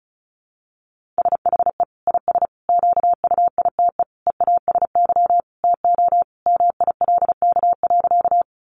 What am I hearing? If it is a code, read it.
Morse code, heard as a keyed tone, about 34 wpm, 700 Hz. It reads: SHE IS QUITE EASY TO MILK.
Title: Sentences from Top 1500 Words 35wpm